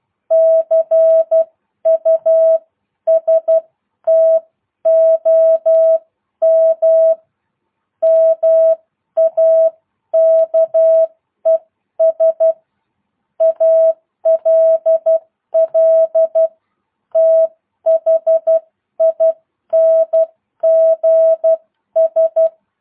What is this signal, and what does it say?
Morse code, heard as a keyed tone, about 12 wpm, spaced as for 10 wpm, 650 Hz. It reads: CUSTOM MAKES ALL THINGS